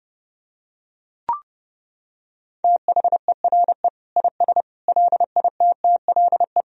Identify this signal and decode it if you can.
Morse code, heard as a keyed tone, about 30 wpm, 700 Hz. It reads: THERE IS LITTLE
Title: Sentences from Top 1000 Words - Review Part 4 - 30wpm